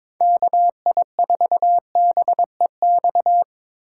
Morse code, heard as a keyed tone, about 21 wpm, 700 Hz. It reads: KI4BEX
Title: USA Call Signs - 6 Characters - 2x Character Spacing - 22wpm